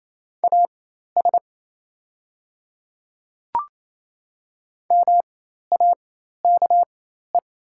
Morse code, heard as a keyed tone, about 28 wpm, spaced as for 13 wpm, 700 Hz. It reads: AS MAKE
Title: 4x Character Spacing - Top 100 Words - 28wpm